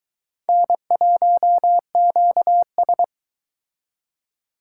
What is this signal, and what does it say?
Morse code, heard as a keyed tone, about 23 wpm, 700 Hz. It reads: N1QS